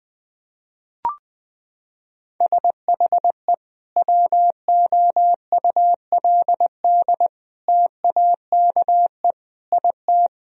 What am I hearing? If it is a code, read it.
Morse code, heard as a keyed tone, about 20 wpm, 700 Hz. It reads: SHE WOULD TAKE IT
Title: Sentences from Top 1000 Words - Review Part 1 - 20wpm